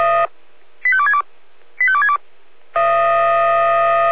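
Five-tone selective call (ZVEI), heard as quick runs of five tones, tones of about 70 ms. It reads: 85262 85262
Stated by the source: Five-tone ZVEI selective call received on an SDR.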